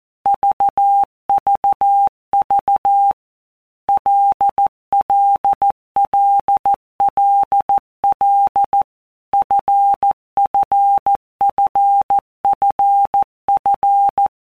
Morse code, heard as a keyed tone, about 14 wpm, 800 Hz. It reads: VVV LLLLL FFFFF
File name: Basic Letter Groups By WPM's / Groups@13 / LFCYQ@13